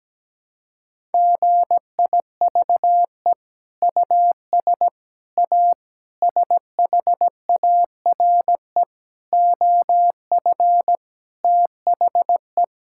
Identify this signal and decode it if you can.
Morse code, heard as a keyed tone, about 17 wpm, 700 Hz. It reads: GIVE US A SHARE OF THE
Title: Sentences from Top 1000 Words - Review Part 6 - 17wpm